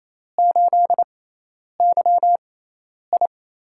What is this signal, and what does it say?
Morse code, heard as a keyed tone, about 28 wpm, spaced as for 10 wpm, 700 Hz. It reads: 8YI
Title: USA Call Signs - 6x Character Spacing - 28wpm